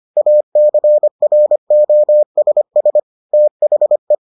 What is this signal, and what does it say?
Morse code, heard as a keyed tone, about 25 wpm, 600 Hz. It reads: ACROSS THE